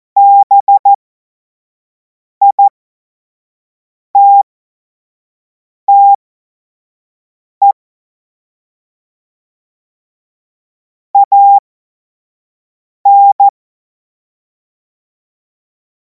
Morse code, heard as a keyed tone, about 14 wpm, spaced as for 5 wpm, 800 Hz. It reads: BITTE AN
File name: Morsekurs / Witze / Witz 10. März 2026 25 F70